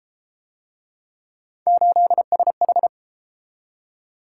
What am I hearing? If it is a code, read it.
Morse code, heard as a keyed tone, about 31 wpm, 700 Hz. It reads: 8SH